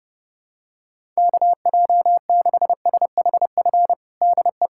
Morse code, heard as a keyed tone, about 31 wpm, 700 Hz. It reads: KJ6SHF DE
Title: USA Callsign DE Callsign 30wpm